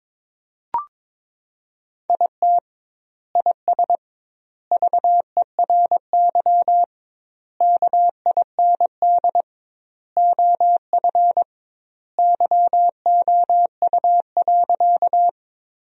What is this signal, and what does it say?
Morse code, heard as a keyed tone, about 21 wpm, 700 Hz. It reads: IT IS VERY KIND OF YOU.
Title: 2x Word-Spacing - Sentences from Top 200 Words - 22wpm